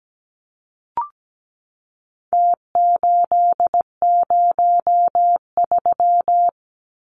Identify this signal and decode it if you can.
Morse code, heard as a keyed tone, about 17 wpm, 700 Hz. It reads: T803
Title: Numbers - Contest Style - 4-Digit 17wpm